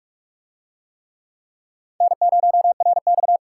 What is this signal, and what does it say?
Morse code, heard as a keyed tone, about 44 wpm, 700 Hz. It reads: N0RX